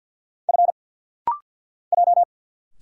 Morse code, heard as a keyed tone, about 50 wpm, 700 Hz. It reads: F J
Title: Single Letter-Number Alt - Rapid-Fire 50wpm